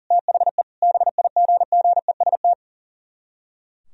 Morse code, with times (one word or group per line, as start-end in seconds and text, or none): 0.1-0.6 THE
0.8-2.5 BIGGEST